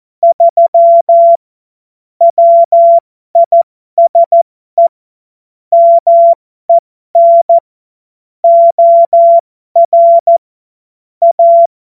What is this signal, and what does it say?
Morse code, heard as a keyed tone, about 14 wpm, spaced as for 12 wpm, 680 Hz. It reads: 3 WISE MEN OR A
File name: Morsekurs / Witze